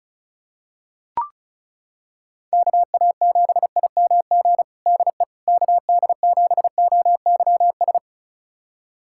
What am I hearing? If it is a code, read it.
Morse code, heard as a keyed tone, about 34 wpm, 700 Hz. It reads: KA7IMG DE KD7OYS
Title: USA Callsign DE Callsign - 6 Characters - 35wpm